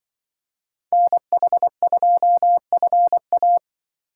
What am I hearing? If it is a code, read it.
Morse code, heard as a keyed tone, about 24 wpm, 700 Hz. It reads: NH2FA
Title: Speed Racing - USA Call Signs 20wpm